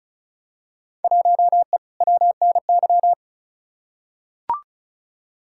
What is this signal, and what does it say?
Morse code, heard as a keyed tone, about 34 wpm, 700 Hz. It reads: 1E WNY
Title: Field Day Exchanges - 35wpm